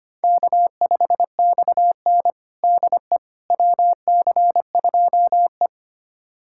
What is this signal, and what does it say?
Morse code, heard as a keyed tone, about 25 wpm, 700 Hz. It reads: K5XN DE WC2E
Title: USA Callsign DE Callsign - 4 Characters - 25wpm